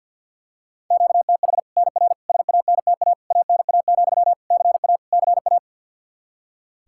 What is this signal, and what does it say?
Morse code, heard as a keyed tone, about 46 wpm, 700 Hz. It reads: QTH NR SANTA ANA, CA CA
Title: Practice QSO - Exchange 1 - Beginning - 50wpm